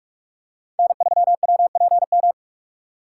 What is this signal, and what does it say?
Morse code, heard as a keyed tone, about 46 wpm, 690 Hz. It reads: N3WPM
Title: USA Call Signs - 5 Characters - 2x Character Spacing - 45wpm